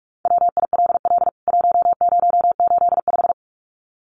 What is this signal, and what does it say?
Morse code, heard as a keyed tone, about 44 wpm, 700 Hz. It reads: WILL 1085
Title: CWT Exchange 45wpm